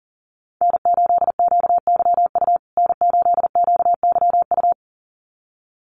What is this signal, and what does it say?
Morse code, heard as a keyed tone, about 40 wpm, 700 Hz. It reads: N8QYU N8QYU
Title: USA Call Signs Repeated 2x 40wpm